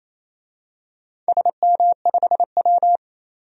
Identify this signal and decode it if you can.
Morse code, heard as a keyed tone, about 28 wpm, 700 Hz. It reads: SM5W
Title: DXCC-100 Call Signs - 4 Characters - 28wpm